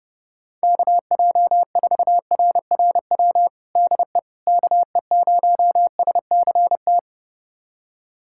Morse code, heard as a keyed tone, about 30 wpm, 700 Hz. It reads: KJ4RRW DE KE0SCT
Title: USA Callsign DE Callsign - 6 Characters - 30wpm